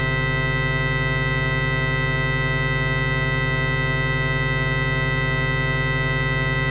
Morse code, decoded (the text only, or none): none